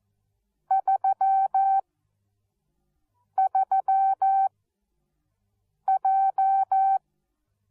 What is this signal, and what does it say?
Morse code, heard as a keyed tone, about 14 wpm, 770 Hz. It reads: 3 3 J